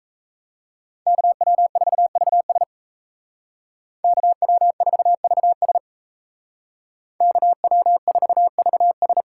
Morse code, heard as a keyed tone, about 36 wpm, 700 Hz. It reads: KW4VS KW4VS KW4VS